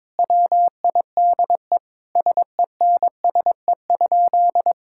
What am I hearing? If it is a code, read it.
Morse code, heard as a keyed tone, about 22 wpm, 700 Hz. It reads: WIDE SENSE?
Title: Sentences from Top 1000 Words - Review Part 6 - 22wpm